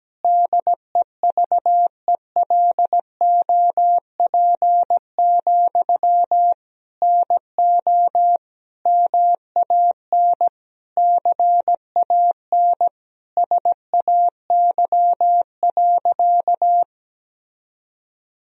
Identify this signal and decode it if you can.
Morse code, heard as a keyed tone, about 17 wpm, 700 Hz. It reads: DEVELOP, NO MAN CAN SAY.
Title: Sentences from Top 1000 Words - Review Part 2 - 17wpm